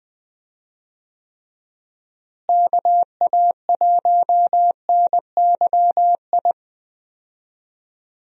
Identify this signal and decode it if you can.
Morse code, heard as a keyed tone, about 20 wpm, 700 Hz. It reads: KA1NYI
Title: USA Call Signs - 6 Characters - Repeated 2x - 20wpm